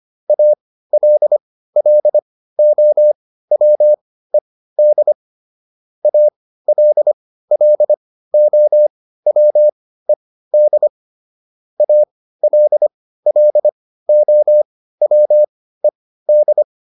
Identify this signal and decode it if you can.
Morse code, heard as a keyed tone, about 25 wpm, spaced as for 15 wpm, 600 Hz. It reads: ALLOWED ALLOWED ALLOWED